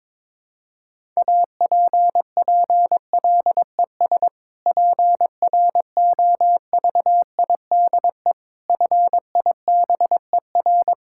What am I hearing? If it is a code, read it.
Morse code, heard as a keyed tone, about 22 wpm, 700 Hz. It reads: APPLES PROVIDE FIBER
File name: Difficult Sentences with Hints - 22wpm